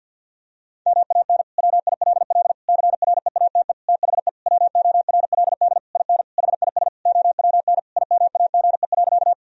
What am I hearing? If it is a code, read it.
Morse code, heard as a keyed tone, about 52 wpm, 700 Hz. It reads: MAN WILL CREATE THE WORLD IN HIS OWN IMAGE.
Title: Sentences from Top 1000 Words - Review Part 4 - 50wpm